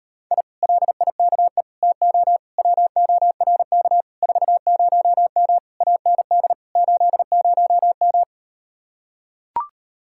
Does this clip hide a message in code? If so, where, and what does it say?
Yes: Morse code from 0.3 to 8.2 s, heard as a keyed tone, about 39 wpm, 700 Hz. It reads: I LIKE TO WORK 40M AND 80M